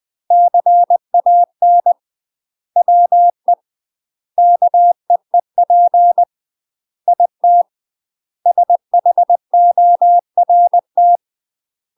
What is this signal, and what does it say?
Morse code, heard as a keyed tone, about 20 wpm, 700 Hz. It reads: CAN WE KEEP IT SHORT
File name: UK MASTER QSOs TRAINING / 20WPM / UK MASTER QSOs 0027